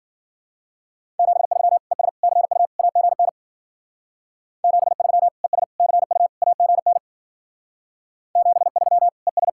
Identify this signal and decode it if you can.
Morse code, heard as a keyed tone, about 52 wpm, 700 Hz. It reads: 73 ES CU AGN 73 ES CU AGN 73 ES